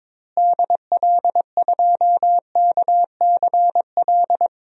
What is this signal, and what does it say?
Morse code, heard as a keyed tone, about 22 wpm, 700 Hz. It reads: DL2KCL
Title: DXCC-100 Call Signs - 6 Characters - Repeated 3x - 22wpm